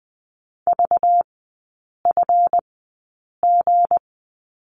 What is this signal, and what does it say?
Morse code, heard as a keyed tone, about 20 wpm, spaced as for 8 wpm, 700 Hz. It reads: VFG
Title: 191 Lesson 31e - Letter v - Three Characters 20wpm